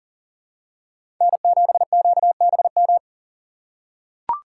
Morse code, heard as a keyed tone, about 37 wpm, 700 Hz. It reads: N7QBM
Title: USA Call Signs - 5 Characters - 3x Character Spacing - 40wpm